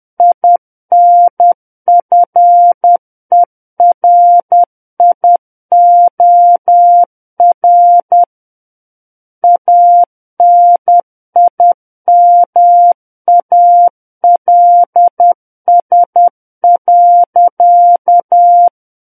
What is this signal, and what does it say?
Morse code, heard as a keyed tone, about 10 wpm, 700 Hz. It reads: INFERIOR ANIMALS.
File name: Book One / 01-Chapter 1 - 10 WPM